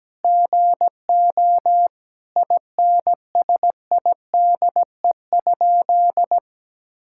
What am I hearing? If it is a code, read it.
Morse code, heard as a keyed tone, about 17 wpm, 700 Hz. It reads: GO INSIDE?
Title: Sentences from Top 1000 Words - Review Part 6 - 17wpm